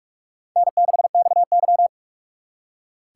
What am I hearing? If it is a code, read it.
Morse code, heard as a keyed tone, about 59 wpm, 700 Hz. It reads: N6XY